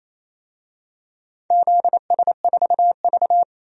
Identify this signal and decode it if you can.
Morse code, heard as a keyed tone, about 28 wpm, 700 Hz. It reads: ZS4V
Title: DXCC-100 Call Signs - 4 Characters - 28wpm